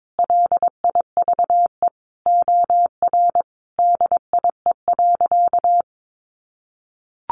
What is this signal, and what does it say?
Morse code, heard as a keyed tone, about 22 wpm, 700 Hz. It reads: LIVE OR DIE.
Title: Sentences from Top 1300 Words 22wpm